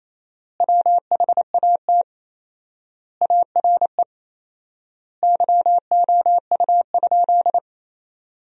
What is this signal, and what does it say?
Morse code, heard as a keyed tone, about 28 wpm, 700 Hz. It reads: WHAT ARE YOU?
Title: 4x Word-Spacing - Sentences from Top 100 Words - 28wpm